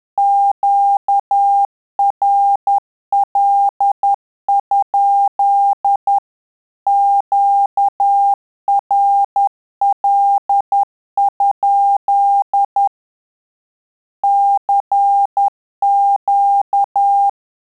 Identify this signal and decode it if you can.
Morse code, heard as a keyed tone, about 10 wpm, 800 Hz. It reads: QRL? QRL? CQ
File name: Telegraphy-group / QSO CW 50 CPM